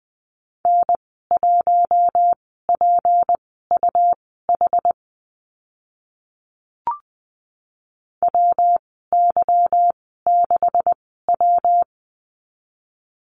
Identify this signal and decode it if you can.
Morse code, heard as a keyed tone, about 20 wpm, spaced as for 14 wpm, 700 Hz. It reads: N1PUH WY6W
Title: 167 Lesson 27g - Number 7 - Call Signs 20wpm